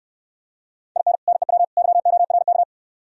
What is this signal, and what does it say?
Morse code, heard as a keyed tone, about 67 wpm, 700 Hz. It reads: IN NEW YORK